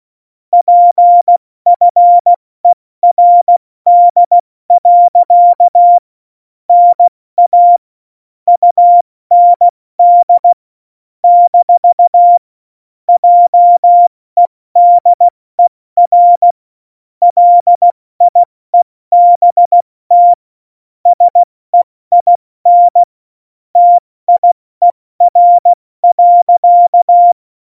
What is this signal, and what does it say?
Morse code, heard as a keyed tone, about 16 wpm, spaced as for 14 wpm, 700 Hz. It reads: PFERD. NA UND - JEDER LIEBT SEIN TIER.
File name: Morsekurs / Witze / Witz 05 Mai, 2020 70 F80